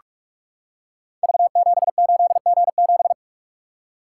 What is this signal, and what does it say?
Morse code, heard as a keyed tone, about 46 wpm, 700 Hz. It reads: V78G7
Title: Code Groups - 5 Letter:Numbers - 45wpm